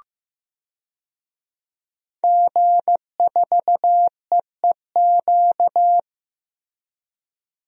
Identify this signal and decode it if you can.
Morse code, heard as a keyed tone, about 15 wpm, 700 Hz. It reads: G4EEQ
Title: DXCC-100 Call Signs - 5 Characters - Repeated 2x - 15wpm